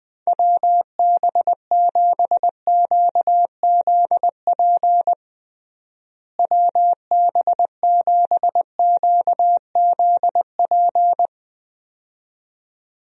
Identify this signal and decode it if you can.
Morse code, heard as a keyed tone, about 20 wpm, 700 Hz. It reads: WB7QZP WB7QZP
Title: USA Call Signs - 6 Characters - Repeated 2x - 20wpm